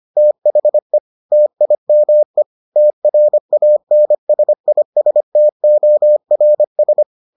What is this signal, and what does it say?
Morse code, heard as a keyed tone, about 25 wpm, 600 Hz. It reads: THE TIME TRANSISTORS